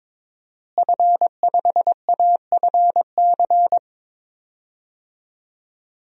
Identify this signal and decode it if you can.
Morse code, heard as a keyed tone, about 22 wpm, 700 Hz. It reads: F5AFC
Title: DXCC-100 Call Signs - 5 Characters - Repeated 3x - 22wpm